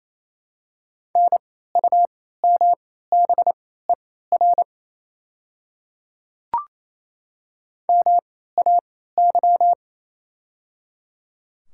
Morse code, heard as a keyed tone, about 27 wpm, spaced as for 16 wpm, 700 Hz. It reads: NUMBER MAY
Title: 3x Character Spacing - Top 100 Words - 28wpm